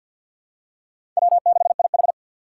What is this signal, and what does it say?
Morse code, heard as a keyed tone, about 51 wpm, 700 Hz. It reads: W6IH
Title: USA Call Signs - 4 Characters - 2x Character Spacing - 50wpm